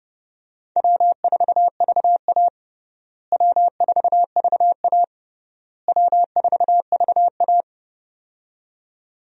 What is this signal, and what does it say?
Morse code, heard as a keyed tone, about 30 wpm, 700 Hz. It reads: W4VA W4VA W4VA
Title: USA Call Signs - 4 Characters - Repeated 3x - 30wpm